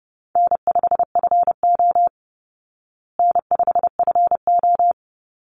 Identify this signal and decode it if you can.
Morse code, heard as a keyed tone, about 30 wpm, 700 Hz. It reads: N5FO N5FO
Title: Call Signs Repeated 3x - 30wpm